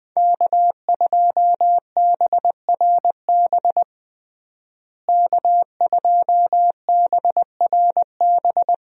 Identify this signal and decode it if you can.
Morse code, heard as a keyed tone, about 20 wpm, 700 Hz. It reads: K2BRB K2BRB
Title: USA Call Signs - 5 Characters - Repeated 2x - 20wpm